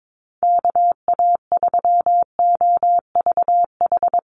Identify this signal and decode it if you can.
Morse code, heard as a keyed tone, about 22 wpm, 700 Hz. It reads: KA3OVH
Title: USA Call Signs Repeated 2x 22wpm